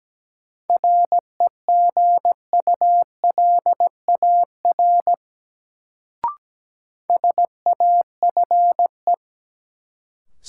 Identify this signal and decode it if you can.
Morse code, heard as a keyed tone, about 17 wpm, 700 Hz. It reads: REGULAR SAFE
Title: Sentences from Top 900 Words - New Word Drill - 17wpm